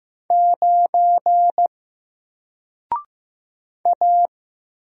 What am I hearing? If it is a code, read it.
Morse code, heard as a keyed tone, about 15 wpm, 700 Hz. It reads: 9 A
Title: Single Letter-Number 15wpm